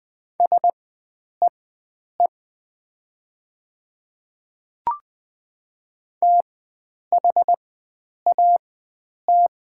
Morse code, heard as a keyed tone, about 20 wpm, spaced as for 9 wpm, 690 Hz. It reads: SEE THAT
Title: 4x Character Spacing - Top 100 Words - 20wpm